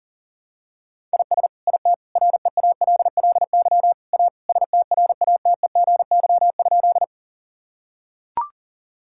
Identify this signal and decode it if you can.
Morse code, heard as a keyed tone, about 40 wpm, 700 Hz. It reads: IS IT REALLY A STRATEGY?
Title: Sentences from Top 1000 Words - Review Part 5 - 40wpm